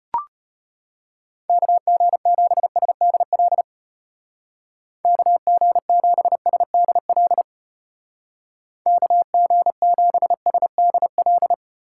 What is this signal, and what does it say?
Morse code, heard as a keyed tone, about 34 wpm, 700 Hz. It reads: KG7SDL KG7SDL KG7SDL